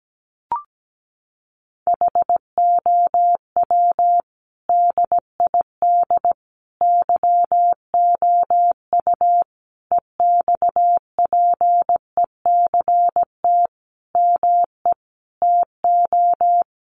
Morse code, heard as a keyed tone, about 17 wpm, 700 Hz. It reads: HOW DID YOU EXPECT ME TO
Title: Sentences from Top 1100 Words 17wpm